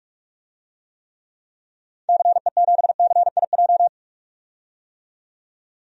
Morse code, heard as a keyed tone, about 46 wpm, 700 Hz. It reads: CE7CIJ
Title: DXCC-100 Call Signs - 6 Characters - 45wpm